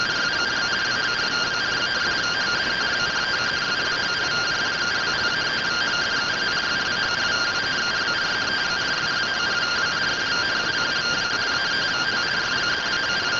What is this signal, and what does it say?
Two-tone teleprinter signal, about 170 Hz shift.
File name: findings / AXIAL fr45 3mm.gen-02 rtty rx